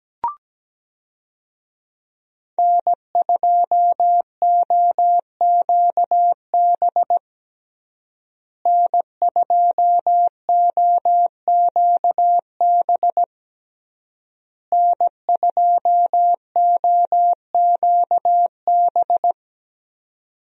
Morse code, heard as a keyed tone, about 17 wpm, 700 Hz. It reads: N2OQB N2OQB N2OQB
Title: USA Call Signs - 5 Characters - Repeated 3x - 17wpm